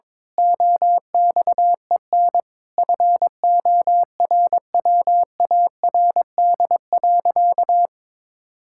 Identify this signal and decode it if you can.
Morse code, heard as a keyed tone, about 22 wpm, 700 Hz. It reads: OXEN FORWARD.
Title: Pangram Challenge v2 - 22wpm